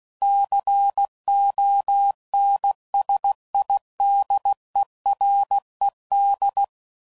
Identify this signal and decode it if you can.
Morse code, heard as a keyed tone, about 16 wpm, 800 Hz. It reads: CONSIDERED
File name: CW Code Groups @15WPM / Technical Subjects / XL